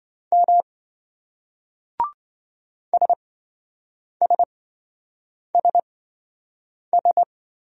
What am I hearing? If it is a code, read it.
Morse code, heard as a keyed tone, about 25 wpm, spaced as for 7 wpm, 700 Hz. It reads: M SSSS